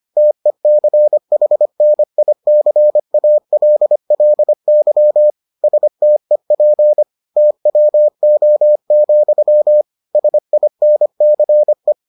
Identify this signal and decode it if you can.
Morse code, heard as a keyed tone, about 25 wpm, 600 Hz. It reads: TECHNICALLY STEP TWO, SINCE